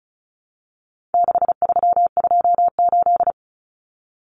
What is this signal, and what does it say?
Morse code, heard as a keyed tone, about 34 wpm, 700 Hz. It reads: 6328
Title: Common Cut Numbers - 4-Digit 35wpm